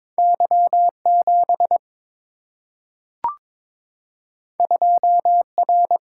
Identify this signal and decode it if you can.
Morse code, heard as a keyed tone, about 22 wpm, 700 Hz. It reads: Y7 2R
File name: practice / 22wpm / Two Letter-Number Combinations - 22wpm